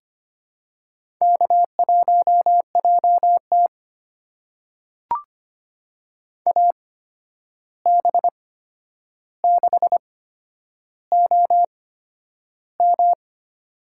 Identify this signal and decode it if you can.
Morse code, heard as a keyed tone, about 25 wpm, spaced as for 10 wpm, 700 Hz. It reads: K1JT AB6OM
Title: USA Call Signs - 8x Character Spacing - 25wpm